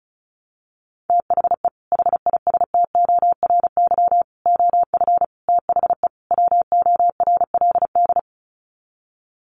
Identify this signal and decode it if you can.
Morse code, heard as a keyed tone, about 34 wpm, 700 Hz. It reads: THE HISTORY OF THE WORLD
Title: Sets of 5 Words Encore2 35wpm